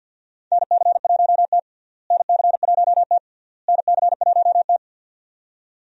Morse code, heard as a keyed tone, about 52 wpm, 700 Hz. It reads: NC1T NC1T NC1T
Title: USA Call Signs - 4 Characters - Repeated 3x - 50wpm